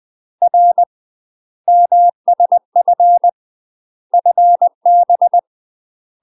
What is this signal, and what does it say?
Morse code, heard as a keyed tone, about 20 wpm, 700 Hz. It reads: R MSF FB
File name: UK MASTER QSOs TRAINING / 20WPM / UK MASTER QSOs 0056